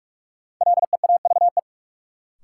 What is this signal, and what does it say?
Morse code, heard as a keyed tone, about 42 wpm, 700 Hz. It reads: LEAVE
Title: Sentences from Top 400 Words - New Word Drill - 45wpm